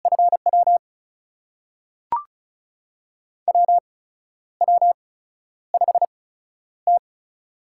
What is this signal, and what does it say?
Morse code, heard as a keyed tone, about 35 wpm, spaced as for 12 wpm, 700 Hz. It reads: FW WW5T
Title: USA Call Signs - 4 Characters - 8x Character Spacing - 35wpm